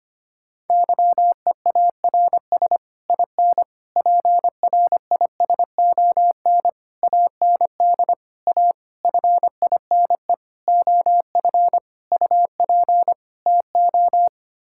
Morse code, heard as a keyed tone, about 25 wpm, 700 Hz. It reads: YEARS IN PRISON AND A FINE OF UP TO
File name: Sets of 9 Words - 25wpm